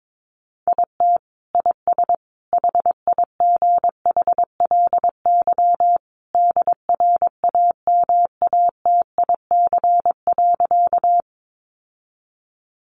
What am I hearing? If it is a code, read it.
Morse code, heard as a keyed tone, about 22 wpm, 700 Hz. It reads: IT IS HIGHLY DRAMATIC.
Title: Sentences from Top 1300 Words 22wpm